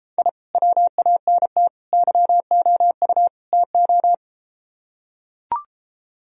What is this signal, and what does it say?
Morse code, heard as a keyed tone, about 33 wpm, 700 Hz. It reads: I WANT YOU TO